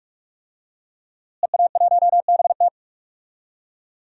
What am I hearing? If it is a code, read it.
Morse code, heard as a keyed tone, about 46 wpm, 700 Hz. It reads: EA1BT
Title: DXCC-100 Call Signs - 5 Characters - Repeated 2x - 45wpm